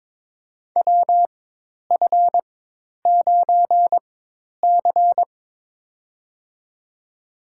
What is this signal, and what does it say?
Morse code, heard as a keyed tone, about 22 wpm, spaced as for 10 wpm, 700 Hz. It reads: WF9C
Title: USA Call Signs - 4x Character Spacing - 22wpm